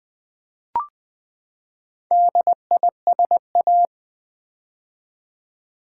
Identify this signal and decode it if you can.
Morse code, heard as a keyed tone, about 20 wpm, 700 Hz. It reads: DISA
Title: Build A Word - 4 Letters At A Time - 8-Letter Words - 20wpm